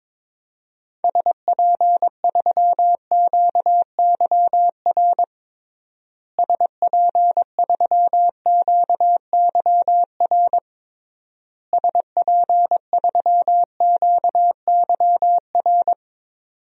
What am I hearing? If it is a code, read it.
Morse code, heard as a keyed tone, about 22 wpm, 700 Hz. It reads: SP3QYR SP3QYR SP3QYR
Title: DXCC-100 Call Signs - 6 Characters - Repeated 3x - 22wpm